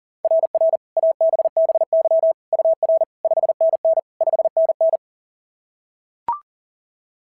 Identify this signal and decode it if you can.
Morse code, heard as a keyed tone, about 37 wpm, 660 Hz. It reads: RR ABBY UR 5NN 5NN
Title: SOTA QSO Practice 40wpm